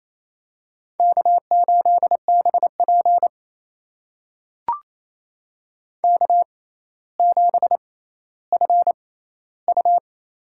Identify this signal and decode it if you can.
Morse code, heard as a keyed tone, about 28 wpm, spaced as for 14 wpm, 700 Hz. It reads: K8BP K7FU
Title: USA Call Signs - 4 Characters - 6x Character Spacing - 28wpm